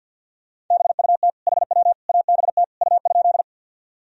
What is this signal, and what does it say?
Morse code, heard as a keyed tone, about 51 wpm, 700 Hz. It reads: BUT HW ABT U?
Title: Sentences from Top 100 Words and Abbreviations 50wpm